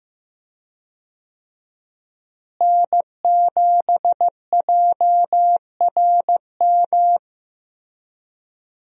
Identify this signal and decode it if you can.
Morse code, heard as a keyed tone, about 15 wpm, 700 Hz. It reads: N7JRM
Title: USA Call Signs - 5 Characters - Repeated 2x - 15wpm